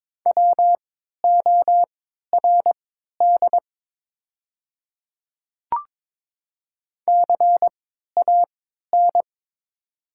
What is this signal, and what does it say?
Morse code, heard as a keyed tone, about 22 wpm, spaced as for 13 wpm, 700 Hz. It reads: WORD CAN
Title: 3x Character Spacing - Top 100 Words - 22wpm